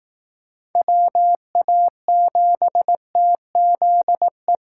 Morse code, heard as a keyed tone, about 18 wpm, 700 Hz. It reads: WA7TZE
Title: Speed Racing - USA Call Signs 15wpm